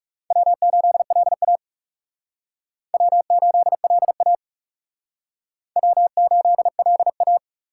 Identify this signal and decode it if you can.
Morse code, heard as a keyed tone, about 37 wpm, 700 Hz. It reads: W8LA W8LA W8LA
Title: Speed Racing - USA 1x2 Call Signs 30wpm